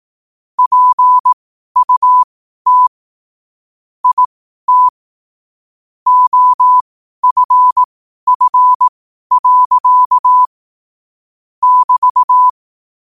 Morse code, heard as a keyed tone, about 18 wpm, spaced as for 12 wpm, 1000 Hz. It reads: PUT IT OFF. <BT>
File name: qotdfiles / 12wpm